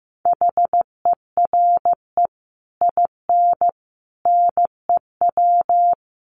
Morse code, heard as a keyed tone, about 14 wpm, 700 Hz. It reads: HERE IN NEW
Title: Sets of 5 Words Encore2 15wpm